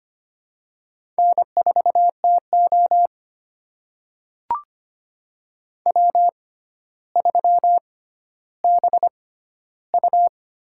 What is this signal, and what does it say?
Morse code, heard as a keyed tone, about 25 wpm, spaced as for 13 wpm, 700 Hz. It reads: N4TO W3BU
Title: USA Call Signs - 4 Characters - 6x Character Spacing - 25wpm